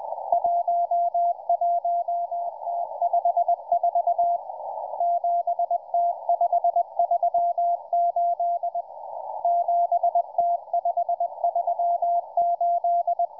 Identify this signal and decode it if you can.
Morse code, heard as a keyed tone, about 20 wpm, 690 Hz, with radio noise. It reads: 11T54 7T538 7T538